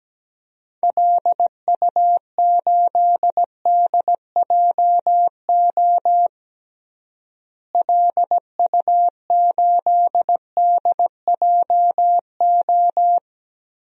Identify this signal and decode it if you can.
Morse code, heard as a keyed tone, about 17 wpm, 700 Hz. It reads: LU8DJO LU8DJO
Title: DXCC-100 Call Signs - 6 Characters - Repeated 3x - 17wpm